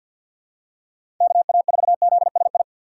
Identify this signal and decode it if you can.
Morse code, heard as a keyed tone, about 48 wpm, 700 Hz. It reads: KA4ZSI
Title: USA Call Signs - 6 Characters - 6x Character Spacing - 50wpm